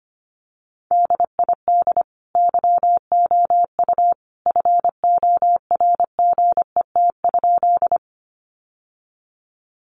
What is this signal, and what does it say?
Morse code, heard as a keyed tone, about 25 wpm, 700 Hz. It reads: DID YOU FORGET?
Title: Sentences from Top 1200 Words 25wpm